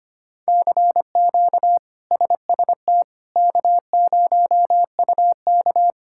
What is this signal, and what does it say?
Morse code, heard as a keyed tone, about 25 wpm, 700 Hz. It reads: CQ SST K0UK
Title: K1USN SST QSO Practice - 25wpm